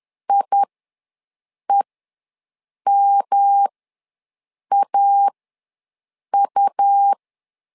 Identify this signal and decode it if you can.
Morse code, heard as a keyed tone, about 10 wpm, spaced as for 6 wpm, 800 Hz. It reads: IEMAU
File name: LEZIONI / 3 MWRO / 3 M ws 10wpm 2121